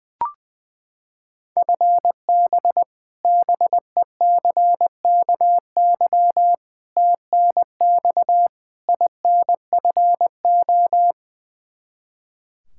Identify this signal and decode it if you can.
Morse code, heard as a keyed tone, about 21 wpm, 700 Hz. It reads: FB BECKY TNX INFO
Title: Practice QSO - Exchange 3 - More Info - 20wpm